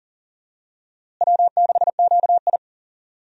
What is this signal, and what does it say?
Morse code, heard as a keyed tone, about 38 wpm, 700 Hz. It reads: W6QI
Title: USA Call Signs - 4 Characters - Repeated 3x - 40wpm